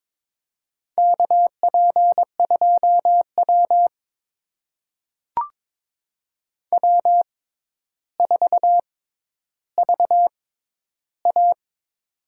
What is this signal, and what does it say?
Morse code, heard as a keyed tone, about 23 wpm, spaced as for 11 wpm, 700 Hz. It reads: KP2W W4VA
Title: USA Call Signs - 4 Characters - 6x Character Spacing - 22wpm